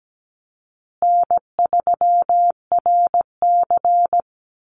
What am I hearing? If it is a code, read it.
Morse code, heard as a keyed tone, about 17 wpm, 700 Hz. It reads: N3RC
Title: Call Signs Repeated 2x - 17wpm